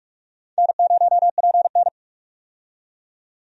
Morse code, heard as a keyed tone, about 48 wpm, 700 Hz. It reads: N0PN
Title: USA Call Signs - 4 Characters - 45wpm